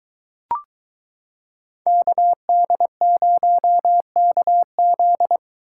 Morse code, heard as a keyed tone, about 22 wpm, 700 Hz. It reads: KD0KZ